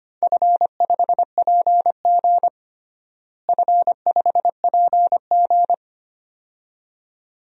Morse code, heard as a keyed tone, about 25 wpm, 700 Hz. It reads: F5PG F5PG
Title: DXCC-100 Call Signs - 4 Characters - Repeated 2x - 25wpm